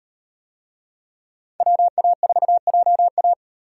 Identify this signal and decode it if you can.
Morse code, heard as a keyed tone, about 38 wpm, 700 Hz. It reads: WA4JA